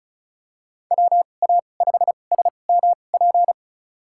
Morse code, heard as a keyed tone, about 36 wpm, spaced as for 25 wpm, 700 Hz. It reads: WA5SMP
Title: USA Call Signs - 6 Characters - 2x Character Spacing - 35wpm